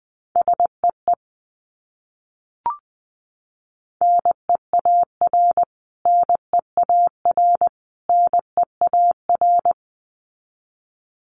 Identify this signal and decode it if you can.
Morse code, heard as a keyed tone, about 20 wpm, 700 Hz. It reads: SEE NEAR NEAR NEAR
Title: Top 300 Words Repeated 20wpm